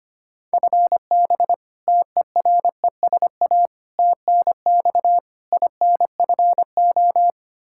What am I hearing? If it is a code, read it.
Morse code, heard as a keyed tone, about 25 wpm, 700 Hz. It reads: FB TERESA TNX INFO
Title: Practice QSO - Exchange 3 - More Info - 25wpm